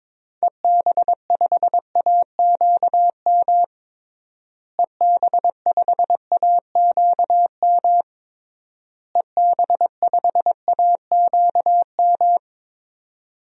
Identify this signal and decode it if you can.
Morse code, heard as a keyed tone, about 21 wpm, 700 Hz. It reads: EB5AQM EB5AQM EB5AQM
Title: DXCC-100 Call Signs - 6 Characters - Repeated 3x - 22wpm